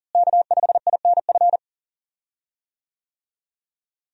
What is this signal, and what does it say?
Morse code, heard as a keyed tone, about 40 wpm, 700 Hz. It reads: K5INF
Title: USA Call Signs - 5 Characters - Repeated 3x - 40wpm